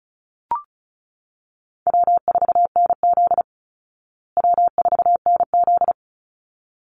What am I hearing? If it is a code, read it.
Morse code, heard as a keyed tone, about 34 wpm, 700 Hz. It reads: W4NZ W4NZ
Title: Easy Call Signs Repeated 2x - 35wpm